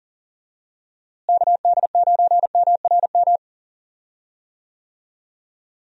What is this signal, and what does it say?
Morse code, heard as a keyed tone, about 37 wpm, 700 Hz. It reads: KD9MRM
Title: USA Call Signs - 6 Characters - Repeated 2x - 40wpm